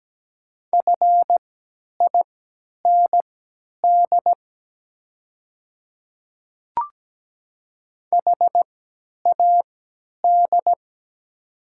Morse code, heard as a keyed tone, about 17 wpm, spaced as for 10 wpm, 700 Hz. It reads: FIND HAD
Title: 3x Character Spacing - Top 300 Words - 17wpm